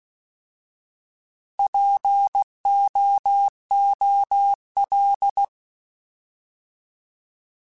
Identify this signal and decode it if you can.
Morse code, heard as a keyed tone, about 16 wpm, 800 Hz. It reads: POOL
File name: Alphabetical / Alphabetical mp3s / Ps @15